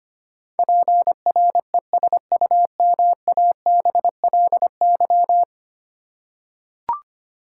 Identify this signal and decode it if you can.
Morse code, heard as a keyed tone, about 25 wpm, 700 Hz. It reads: PRESUMABLY
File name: Ten Letter Words - 25wpm